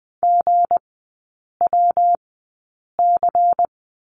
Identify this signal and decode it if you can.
Morse code, heard as a keyed tone, about 20 wpm, spaced as for 8 wpm, 700 Hz. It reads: GWC
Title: 198 Lesson 32e - Letter k - Three Characters 20wpm